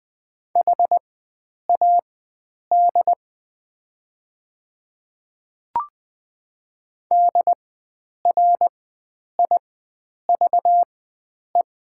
Morse code, heard as a keyed tone, about 20 wpm, spaced as for 9 wpm, 700 Hz. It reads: HAD DRIVE
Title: 4x Character Spacing - Top 500 Words - 20wpm